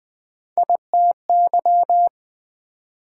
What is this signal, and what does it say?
Morse code, heard as a keyed tone, about 27 wpm, spaced as for 24 wpm, 700 Hz. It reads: ITY
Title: Speed-Racing Multi-letter Phonemes - 20wpm